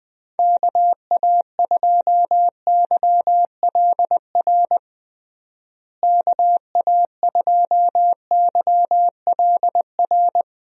Morse code, heard as a keyed tone, about 20 wpm, 700 Hz. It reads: KA2YLR KA2YLR
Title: USA Call Signs - 6 Characters - Repeated 3x - 20wpm